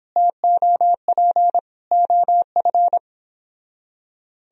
Morse code, heard as a keyed tone, about 31 wpm, 700 Hz. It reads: TOP OF